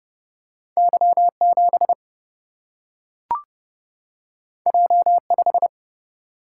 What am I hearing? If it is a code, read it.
Morse code, heard as a keyed tone, about 31 wpm, 700 Hz. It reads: Y7 J5
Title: Single-Letter Single-Number 30wpm